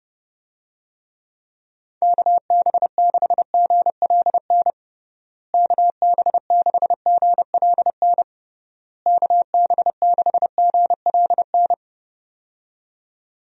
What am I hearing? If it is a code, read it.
Morse code, heard as a keyed tone, about 32 wpm, 700 Hz. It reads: KB6GLN KB6GLN KB6GLN
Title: USA Call Signs - 6 Characters - Repeated 3x - 30wpm